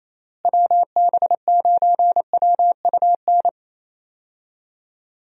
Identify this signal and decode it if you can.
Morse code, heard as a keyed tone, about 27 wpm, 700 Hz. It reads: WB9WUN
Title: USA Call Signs - 6 Characters - Repeated 2x - 28wpm